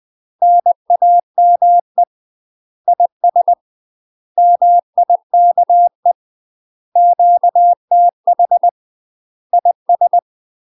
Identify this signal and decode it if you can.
Morse code, heard as a keyed tone, about 20 wpm, 700 Hz. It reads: NAME IS MIKE QTH IS
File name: UK MASTER QSOs TRAINING / 20WPM / UK MASTER QSOs 0004